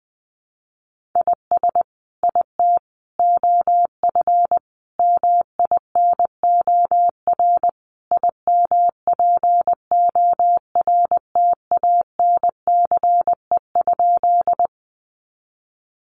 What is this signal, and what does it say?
Morse code, heard as a keyed tone, about 20 wpm, 700 Hz. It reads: IS IT OF MINOR IMPORTANCE?
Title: Sentences from Top 1500 Words 20wpm